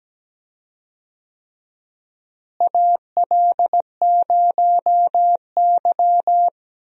Morse code, heard as a keyed tone, about 17 wpm, 700 Hz. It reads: AL0Y
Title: USA Call Signs - 4 Characters - Repeated 3x - 17wpm